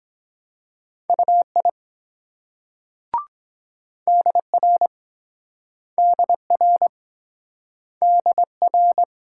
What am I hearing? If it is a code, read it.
Morse code, heard as a keyed tone, about 23 wpm, 700 Hz. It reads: UI DR DR DR